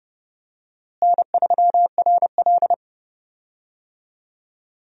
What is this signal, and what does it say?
Morse code, heard as a keyed tone, about 30 wpm, 700 Hz. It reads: N3RL
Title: USA Call Signs - 4 Characters - Repeated 3x - 30wpm